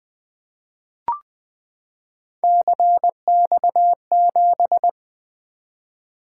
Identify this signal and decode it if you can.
Morse code, heard as a keyed tone, about 20 wpm, 700 Hz. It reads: CX7
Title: Three Letter-Number Combinations - 20wpm